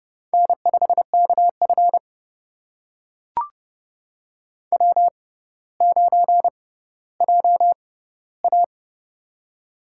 Morse code, heard as a keyed tone, about 30 wpm, spaced as for 15 wpm, 700 Hz. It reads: N5KF W9JA
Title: USA Call Signs - 4 Characters - 6x Character Spacing - 30wpm